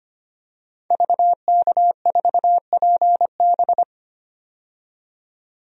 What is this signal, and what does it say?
Morse code, heard as a keyed tone, about 25 wpm, 700 Hz. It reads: VK4PB
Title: DXCC-100 Call Signs - 5 Characters - 25wpm